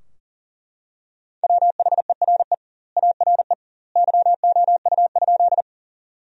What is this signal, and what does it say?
Morse code, heard as a keyed tone, about 40 wpm, 700 Hz. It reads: WHERE ARE YOU?
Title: 2x Word-Spacing - Sentences from Top 200 Words - 40wpm